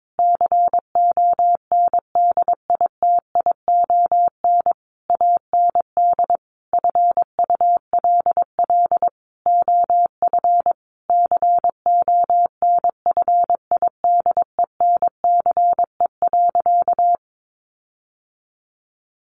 Morse code, heard as a keyed tone, about 22 wpm, 700 Hz. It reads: CONDITION AND FULL OF CONFIDENCE.
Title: Sentences from Top 1500 Words 22wpm